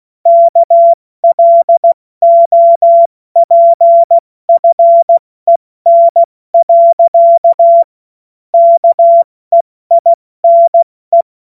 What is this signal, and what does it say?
Morse code, heard as a keyed tone, about 16 wpm, 680 Hz. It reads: KLOPFEN. KEINE
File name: Witz-02-September-2025-70-F80